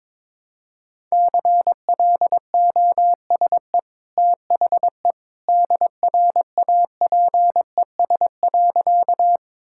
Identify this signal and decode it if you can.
Morse code, heard as a keyed tone, about 23 wpm, 700 Hz. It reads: CLOSE THE DRAPES.
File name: practice / 22wpm / Easy Sentences with Hints - 22wpm